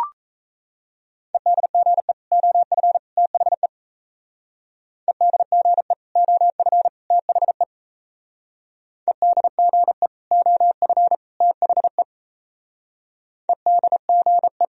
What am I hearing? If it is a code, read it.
Morse code, heard as a keyed tone, about 34 wpm, 700 Hz. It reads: EDGE OF THE EDGE OF THE EDGE OF THE EDGE
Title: Speed-Racing - Sets of 3 Words 28wpm